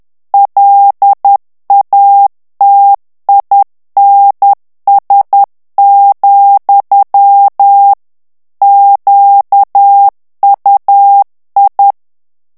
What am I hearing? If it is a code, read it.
Morse code, heard as a keyed tone, about 10 wpm, 800 Hz. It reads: LATINS, QUI
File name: morse mp3 / FRANCAIS TEXTE CLAIR 2 VITESSE 10 MOTS MINUTES